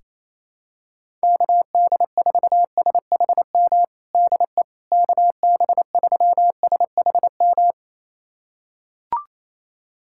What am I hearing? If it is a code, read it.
Morse code, heard as a keyed tone, about 28 wpm, 700 Hz. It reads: KD4SHM DE KB3SHM
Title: USA Callsign DE Callsign - 6 Characters - 28wpm